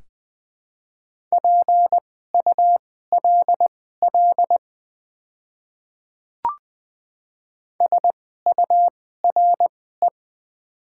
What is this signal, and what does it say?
Morse code, heard as a keyed tone, about 20 wpm, spaced as for 15 wpm, 700 Hz. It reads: PULL SURE
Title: 2x Character Spacing - Top 500 Words - 20wpm